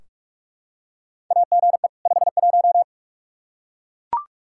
Morse code, heard as a keyed tone, about 44 wpm, 700 Hz. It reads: AGE 51
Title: Practice QSO - Exchange 3 - More Info - 45wpm